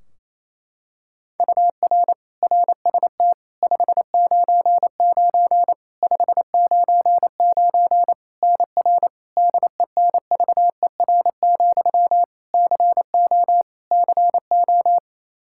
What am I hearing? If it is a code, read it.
Morse code, heard as a keyed tone, about 27 wpm, 700 Hz. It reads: UR RST 599 599 NR DENVER, CO CO
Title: Practice QSO - Exchange 1 - Beginning - 28wpm